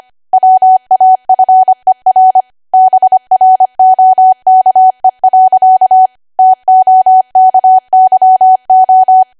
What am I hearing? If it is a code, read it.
Morse code, heard as a keyed tone, about 25 wpm, 750 Hz. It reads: WAFER BROKE. TOKYO
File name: Archive / 25 WPM / 25WPM